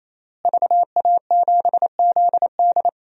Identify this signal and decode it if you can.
Morse code, heard as a keyed tone, about 27 wpm, 700 Hz. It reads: VA7ZD
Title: DXCC-100 Call Signs - 5 Characters - Repeated 3x - 28wpm